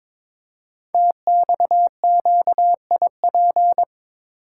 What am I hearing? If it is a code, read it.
Morse code, heard as a keyed tone, about 23 wpm, 700 Hz. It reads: TXQIP
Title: Code Groups - 5 Letters - 22wpm